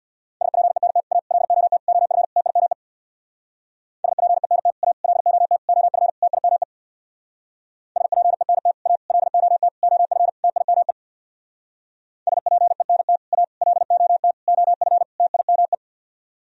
Morse code, heard as a keyed tone, about 64 wpm, 700 Hz. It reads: SPENT A LOT OF TIME SPENT A LOT OF TIME SPENT A LOT OF TIME SPENT A LOT OF TIME